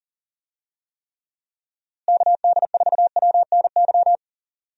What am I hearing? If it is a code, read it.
Morse code, heard as a keyed tone, about 40 wpm, 700 Hz. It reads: KD4WNY
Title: USA Call Signs - 6 Characters - Repeated 2x - 40wpm